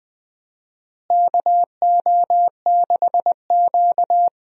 Morse code, heard as a keyed tone, about 20 wpm, 700 Hz. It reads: KO6Q
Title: USA Call Signs - 4 Characters - Repeated 2x - 20wpm